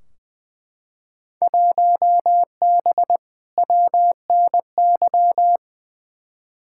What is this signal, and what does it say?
Morse code, heard as a keyed tone, about 21 wpm, 700 Hz. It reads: 1B WNY
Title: Field Day Exchanges Encore - 20wpm